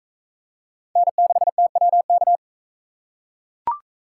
Morse code, heard as a keyed tone, about 40 wpm, 700 Hz. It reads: N6TWK